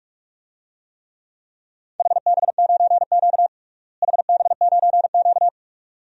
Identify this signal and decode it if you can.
Morse code, heard as a keyed tone, about 46 wpm, 700 Hz. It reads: HB9Q HB9Q
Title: DXCC-100 Call Signs - 4 Characters - Repeated 2x - 45wpm